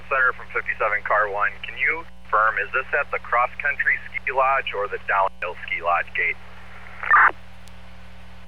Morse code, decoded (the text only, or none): none